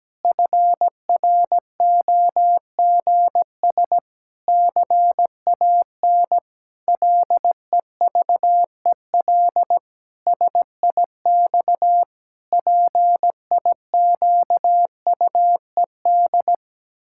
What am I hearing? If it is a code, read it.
Morse code, heard as a keyed tone, about 17 wpm, 700 Hz. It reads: FROGS CAN LEVEL SIX PIQUED
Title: Pangram Challenge v2 - 17wpm